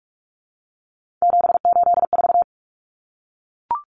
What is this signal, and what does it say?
Morse code, heard as a keyed tone, about 44 wpm, 700 Hz. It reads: 784
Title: Numbers - 3-Digit 45wpm